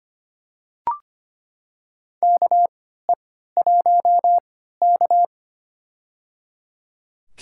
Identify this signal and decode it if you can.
Morse code, heard as a keyed tone, about 25 wpm, spaced as for 14 wpm, 700 Hz. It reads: KE1K
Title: USA Call Signs - 4 Characters - 3x Character Spacing - 25wpm